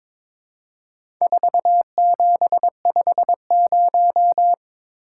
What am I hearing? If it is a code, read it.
Morse code, heard as a keyed tone, about 22 wpm, 700 Hz. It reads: 4750
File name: practice / 22wpm / Four Letter-Number Combinations - 22wpm